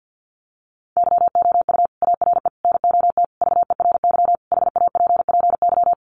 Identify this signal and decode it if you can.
Morse code, heard as a keyed tone, about 50 wpm, 700 Hz. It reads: YOU ARE NOT VERY HAPPY
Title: Sentences from Top 1200 Words 50wpm